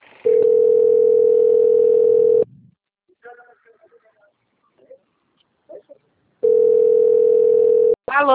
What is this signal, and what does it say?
Ringback tone (North America), 440 + 480 Hz, with speech.
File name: out- - - - - .94517